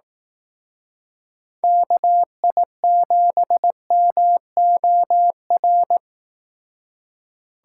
Morse code, heard as a keyed tone, about 18 wpm, 700 Hz. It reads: KI7MOR
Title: Speed Racing - USA 2x3 Call Signs 15wpm